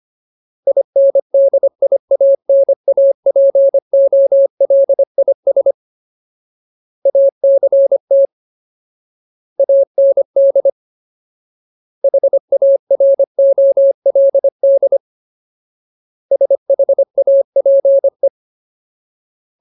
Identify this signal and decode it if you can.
Morse code, heard as a keyed tone, about 25 wpm, 550 Hz. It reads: INDIANAPOLIS ACT AND HAROLD SHAPE
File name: Ultimate Word List / 25WPM Ultimate